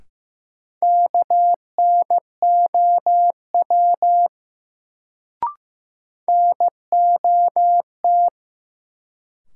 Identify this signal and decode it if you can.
Morse code, heard as a keyed tone, about 15 wpm, 700 Hz. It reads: KNOW NOT
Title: Sets of 2 Words - New Word Drill - 15wpm